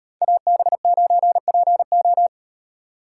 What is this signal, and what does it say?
Morse code, heard as a keyed tone, about 38 wpm, 700 Hz. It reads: AB9PO